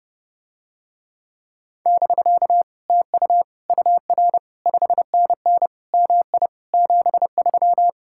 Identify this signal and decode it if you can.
Morse code, heard as a keyed tone, about 30 wpm, 700 Hz. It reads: <BK> TU UR 5NN MI 73